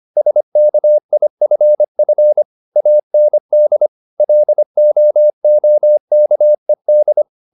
Morse code, heard as a keyed tone, about 25 wpm, 600 Hz. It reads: SKIFF AND LOOKED
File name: the old man and the sea part 1 25wpm